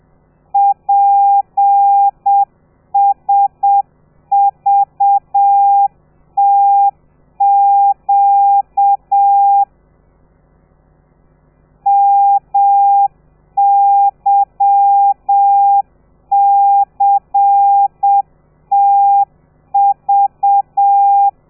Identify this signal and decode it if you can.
Morse code, heard as a keyed tone, about 7 wpm, 800 Hz. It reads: PSVTQ MYCTV
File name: telegrafia / i2tfj / Lez13 30